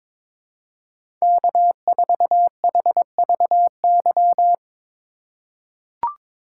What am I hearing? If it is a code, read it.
Morse code, heard as a keyed tone, about 22 wpm, 700 Hz. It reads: K4HVY
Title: USA Call Signs - 5 Characters - 2x Character Spacing - 22wpm